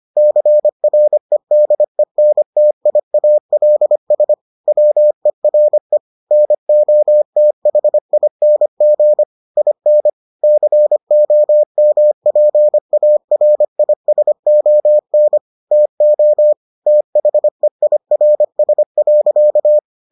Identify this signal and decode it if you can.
Morse code, heard as a keyed tone, about 25 wpm, 600 Hz. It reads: CREDENTIALS WERE NOTHING IN COMPARISON TO THEIRS.